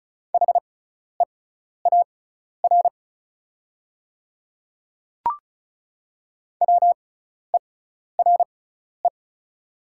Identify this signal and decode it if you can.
Morse code, heard as a keyed tone, about 33 wpm, spaced as for 12 wpm, 690 Hz. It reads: HEAR WERE
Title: 6x Character Spacing - Top 300 Words - 35wpm